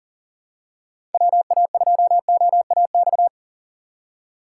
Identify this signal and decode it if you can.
Morse code, heard as a keyed tone, about 40 wpm, 700 Hz. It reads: WA2OAX
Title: USA Call Signs - 6 Characters - 2x Character Spacing - 40wpm